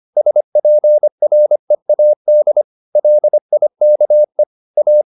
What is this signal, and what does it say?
Morse code, heard as a keyed tone, about 25 wpm, 600 Hz. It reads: SPREAD LIKE A